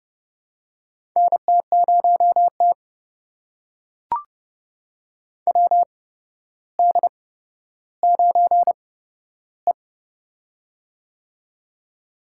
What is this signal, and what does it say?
Morse code, heard as a keyed tone, about 31 wpm, spaced as for 13 wpm, 700 Hz. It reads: NT0T WD9E
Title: USA Call Signs - 8x Character Spacing - 30wpm